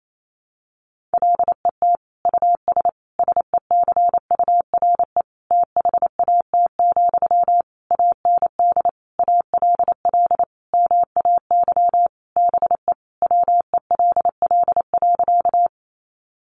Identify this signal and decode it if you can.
Morse code, heard as a keyed tone, about 28 wpm, 700 Hz. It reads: LET US SECURE THAT, AND ALL MAY BE WELL.
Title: Sentences from Top 1200 Words 28wpm